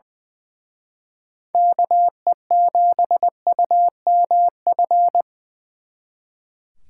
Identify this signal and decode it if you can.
Morse code, heard as a keyed tone, about 24 wpm, 700 Hz. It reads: KE7UMF
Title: Speed Racing - USA 2x3 Call Signs 20wpm